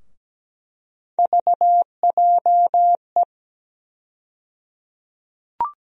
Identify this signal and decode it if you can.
Morse code, heard as a keyed tone, about 17 wpm, 700 Hz. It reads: VJE
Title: USA Call Sign Postfixes - 3 Letters 17wpm